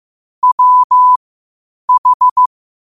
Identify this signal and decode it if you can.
Morse code, heard as a keyed tone, about 15 wpm, spaced as for 8 wpm, 1000 Hz. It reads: WH
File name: qotdfiles / 8wpm